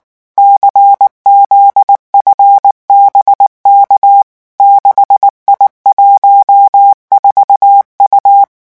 Morse code, heard as a keyed tone, about 19 wpm, 800 Hz. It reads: CZFBK 6I14U
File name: morse mp3 / revision10-vitesse 18 mots